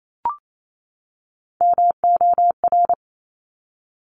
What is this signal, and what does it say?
Morse code, heard as a keyed tone, about 28 wpm, 700 Hz. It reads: MOR
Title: Top Three Letter Combinations in Words 28wpm